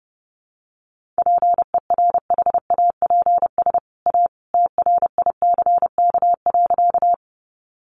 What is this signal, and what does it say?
Morse code, heard as a keyed tone, about 28 wpm, 700 Hz. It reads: PERHAPS A TRICK.
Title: Sentences from Top 1500 Words 30wpm